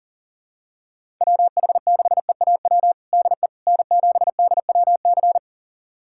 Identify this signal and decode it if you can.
Morse code, heard as a keyed tone, about 37 wpm, 700 Hz. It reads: WH6EAW DE N7DWC
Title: USA Callsign DE Callsign 40wpm